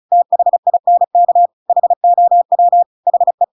Morse code, heard as a keyed tone, about 34 wpm, 700 Hz. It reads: THINK HOW HE
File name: Pinocchio / Pinocchio ch5 35 35 ik1hge